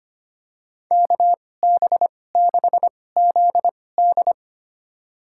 Morse code, heard as a keyed tone, about 25 wpm, spaced as for 18 wpm, 700 Hz. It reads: KB6ZD
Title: USA Call Signs - 5 Characters - 2x Character Spacing - 25wpm